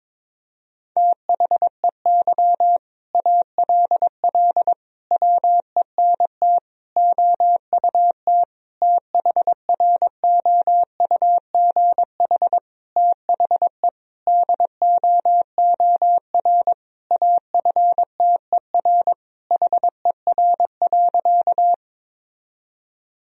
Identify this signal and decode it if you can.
Morse code, heard as a keyed tone, about 22 wpm, 700 Hz. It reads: THEY ALL WENT OUT THROUGH THE DOOR AFTER HER.
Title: Sentences from Top 1000 Words - Review Part 1 - 22wpm